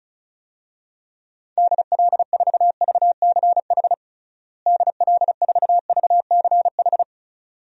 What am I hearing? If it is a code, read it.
Morse code, heard as a keyed tone, about 36 wpm, 700 Hz. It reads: DL4VCH DL4VCH
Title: DXCC-100 Call Signs - 6 Characters - Repeated 2x - 35wpm